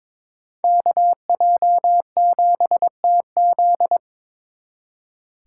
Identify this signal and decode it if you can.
Morse code, heard as a keyed tone, about 26 wpm, 700 Hz. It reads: KJ7TZ
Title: Speed Racing - USA 2x2 Call Signs 22wpm